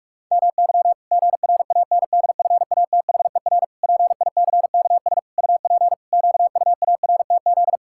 Morse code, heard as a keyed tone, about 41 wpm, 700 Hz. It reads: MY GRANDFATHER PICKS UP QUARTZ
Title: Pangram Challenge v2 - 45wpm